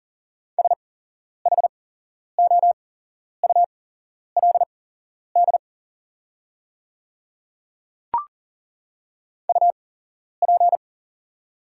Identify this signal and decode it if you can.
Morse code, heard as a keyed tone, about 37 wpm, spaced as for 11 wpm, 690 Hz. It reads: SHOULD UP
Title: 8x Character Spacing - Top 300 Words - 40wpm